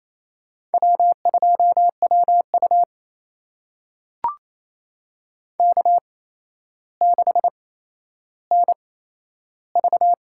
Morse code, heard as a keyed tone, about 28 wpm, spaced as for 12 wpm, 700 Hz. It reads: W2WU K6NV
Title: USA Call Signs - 4 Characters - 8x Character Spacing - 28wpm